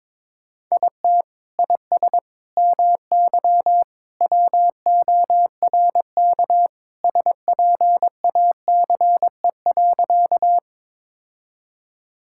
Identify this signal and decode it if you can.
Morse code, heard as a keyed tone, about 22 wpm, 700 Hz. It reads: IT IS MY WORK SPACE.
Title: Sentences from Top 1000 Words - Review Part 5 - 22wpm